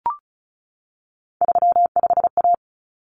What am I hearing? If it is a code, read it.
Morse code, heard as a keyed tone, about 34 wpm, 700 Hz. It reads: 35A
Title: Common Cut Numbers - 3-Digit 35wpm